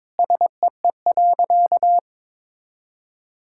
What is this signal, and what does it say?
Morse code, heard as a keyed tone, about 22 wpm, 700 Hz. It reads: SEE.
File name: practice / 22wpm / Sentences with 3-Word Phrases - 22wpm